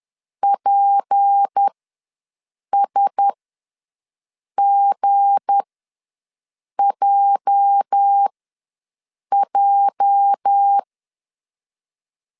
Morse code, heard as a keyed tone, about 10 wpm, spaced as for 6 wpm, 800 Hz. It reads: PSGJJ
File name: LEZIONI / 8 YJ45 / 8 YJ ws 10wpm 4043